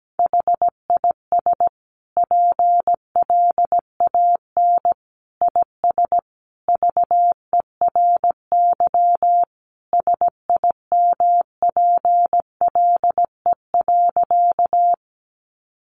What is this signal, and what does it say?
Morse code, heard as a keyed tone, about 17 wpm, 700 Hz. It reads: HIS PLAN IS VERY SIMPLE.
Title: Sentences from Top 500 Words 17wpm